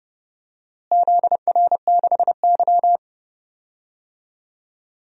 Morse code, heard as a keyed tone, about 30 wpm, 700 Hz. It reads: ZR6Y
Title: DXCC-100 Call Signs - 4 Characters - Repeated 2x - 30wpm